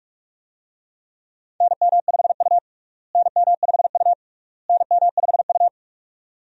Morse code, heard as a keyed tone, about 46 wpm, 700 Hz. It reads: NM5U NM5U NM5U
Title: USA Call Signs - 4 Characters - Repeated 3x - 45wpm